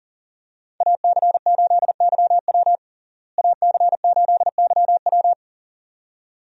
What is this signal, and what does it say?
Morse code, heard as a keyed tone, about 37 wpm, 700 Hz. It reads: AC8YW AC8YW
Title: USA Call Signs - 5 Characters - Repeated 2x - 40wpm